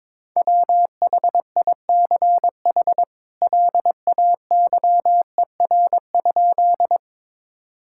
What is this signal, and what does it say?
Morse code, heard as a keyed tone, about 22 wpm, 700 Hz. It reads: WHICH LAYER?
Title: Sentences from Top 1600 Words 22wpm